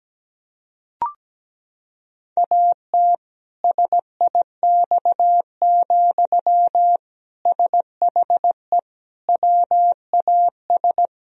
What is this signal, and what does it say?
Morse code, heard as a keyed tone, about 17 wpm, 700 Hz. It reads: AT SIX, SHE WAS
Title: Sentences from Top 1000 Words - Review Part 2 - 17wpm